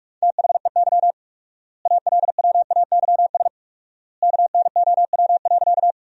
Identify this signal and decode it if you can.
Morse code, heard as a keyed tone, about 46 wpm, 700 Hz. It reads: THEY ALWAYS KNOW.
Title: 4x Word-Spacing - Sentences from Top 300 Words - 45wpm